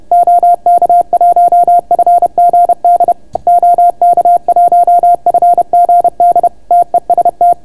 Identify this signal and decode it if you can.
Morse code, heard as a keyed tone, about 31 wpm, 690 Hz. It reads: OK1FGD OK1FGD TEST